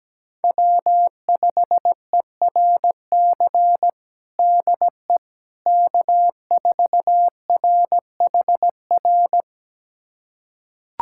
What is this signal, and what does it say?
Morse code, heard as a keyed tone, about 17 wpm, 700 Hz. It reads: W5ERC DE K4RHR
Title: USA Callsign DE Callsign 17wpm